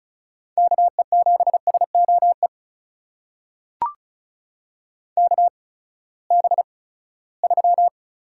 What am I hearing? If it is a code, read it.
Morse code, heard as a keyed tone, about 36 wpm, spaced as for 20 wpm, 700 Hz. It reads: KE7SOE KB3
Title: USA Call Signs - 6 Characters - 8x Character Spacing - 35wpm